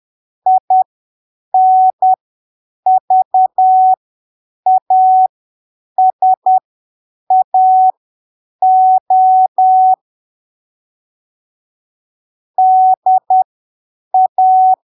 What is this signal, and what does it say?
Morse code, heard as a keyed tone, about 10 wpm, spaced as for 7 wpm, 750 Hz. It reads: INVASAO DA
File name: arquivos / texto / texto 30 10ppm